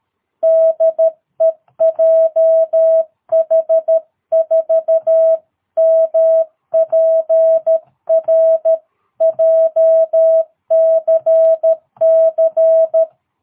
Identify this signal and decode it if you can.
Morse code, heard as a keyed tone, about 13 wpm, 650 Hz. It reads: DE JH4MPR JCC
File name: jh8cbh / 2022cw